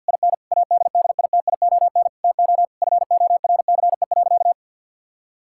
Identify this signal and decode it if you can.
Morse code, heard as a keyed tone, about 52 wpm, 700 Hz. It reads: IN ADDITION TO FORCE.
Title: Sentences from Top 1000 Words - Review Part 6 - 50wpm